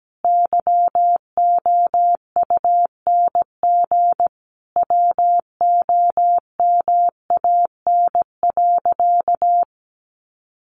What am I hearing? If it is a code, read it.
Morse code, heard as a keyed tone, about 17 wpm, 700 Hz. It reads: YOUNG WOMAN.
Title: Sentences from Top 1400 Words 17wpm